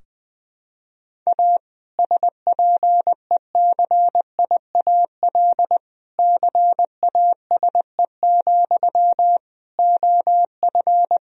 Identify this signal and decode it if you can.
Morse code, heard as a keyed tone, about 20 wpm, 700 Hz. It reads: A SPECIAL CASE, OF
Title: Sentences from Top 1000 Words - Review Part 3 - 20wpm